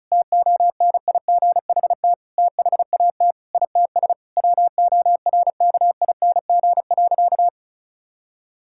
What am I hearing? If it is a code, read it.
Morse code, heard as a keyed tone, about 34 wpm, 700 Hz. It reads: TONIGHT THAT ITS WORKING.
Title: Sentences from Top 1000 Words - Review Part 6 - 35wpm